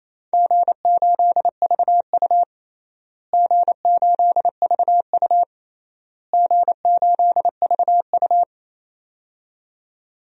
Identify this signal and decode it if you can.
Morse code, heard as a keyed tone, about 27 wpm, 700 Hz. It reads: G8VU G8VU G8VU
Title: DXCC-100 Call Signs - 4 Characters - Repeated 3x - 28wpm